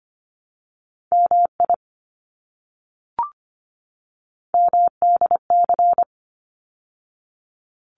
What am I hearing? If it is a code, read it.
Morse code, heard as a keyed tone, about 25 wpm, 700 Hz. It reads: MI MDC
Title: ARRL Sections 25wpm